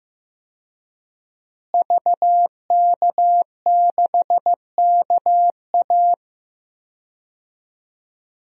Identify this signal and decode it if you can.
Morse code, heard as a keyed tone, about 15 wpm, 700 Hz. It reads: VK6KA
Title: DXCC-100 Call Signs - 5 Characters - Repeated 2x - 15wpm